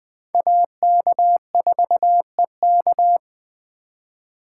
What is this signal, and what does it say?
Morse code, heard as a keyed tone, about 20 wpm, 700 Hz. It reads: AK4EK
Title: USA Call Signs - 5 Characters - 4x Character Spacing - 20wpm